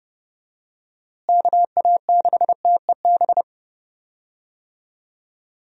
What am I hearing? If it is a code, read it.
Morse code, heard as a keyed tone, about 32 wpm, 700 Hz. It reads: KA6TEB
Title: USA Call Signs - 6 Characters - Repeated 3x - 30wpm